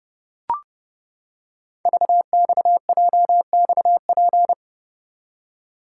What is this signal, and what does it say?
Morse code, heard as a keyed tone, about 30 wpm, 700 Hz. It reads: VXJXP
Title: Code Groups - 5 Letter:Numbers - 30wpm